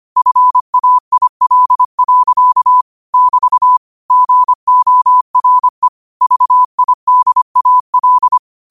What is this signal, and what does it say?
Morse code, heard as a keyed tone, about 25 wpm, 1000 Hz. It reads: FAIL. <BT> GORE VIDAL
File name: qotdfiles / 25wpm